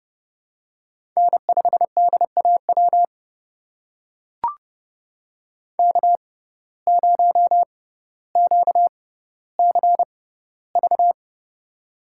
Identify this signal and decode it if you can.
Morse code, heard as a keyed tone, about 31 wpm, spaced as for 16 wpm, 700 Hz. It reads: N5DAW K0QCV
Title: USA Call Signs - 5 Characters - 6x Character Spacing - 30wpm